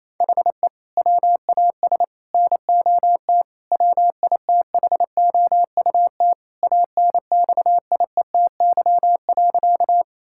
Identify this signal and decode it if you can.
Morse code, heard as a keyed tone, about 28 wpm, 700 Hz. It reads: HE WAS NOT WITHOUT ANXIETY.
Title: Sentences from Top 1600 Words 28wpm